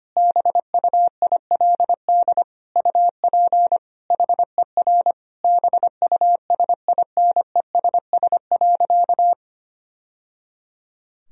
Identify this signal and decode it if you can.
Morse code, heard as a keyed tone, about 25 wpm, 700 Hz. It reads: BUILD UP HER BUSINESS.
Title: Sentences from Top 1000 Words - Review Part 1 - 25wpm